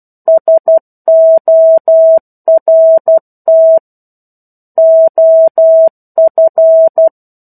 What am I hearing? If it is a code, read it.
Morse code, heard as a keyed tone, about 12 wpm, 650 Hz. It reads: SORT OF
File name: 01-Chapter-01 12 WPM